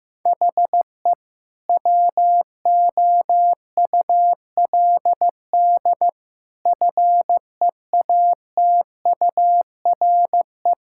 Morse code, heard as a keyed tone, about 16 wpm, 700 Hz. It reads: HE WOULD FEATURE
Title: Sentences from Top 1000 Words - Review Part 5 - 15wpm